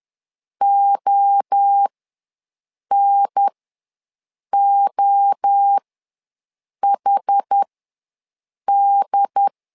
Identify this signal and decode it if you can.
Morse code, heard as a keyed tone, about 10 wpm, spaced as for 6 wpm, 800 Hz. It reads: ONOHD
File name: LEZIONI / 3 MWRO / 3 MWRO ws 10wpm 2310